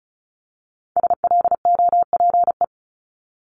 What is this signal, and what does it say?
Morse code, heard as a keyed tone, about 36 wpm, 700 Hz. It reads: SLOPE
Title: Multi-letter Phonemes 35wpm